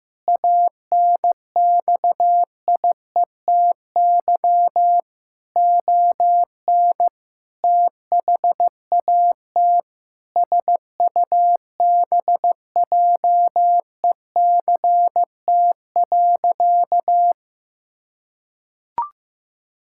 Morse code, heard as a keyed tone, about 16 wpm, 700 Hz. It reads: ANXIETY ON THAT SUBJECT.
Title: Sentences from Top 1600 Words 15wpm